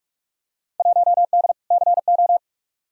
Morse code, heard as a keyed tone, about 42 wpm, 700 Hz. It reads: 1D CO
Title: Field Day Exchanges - 45wpm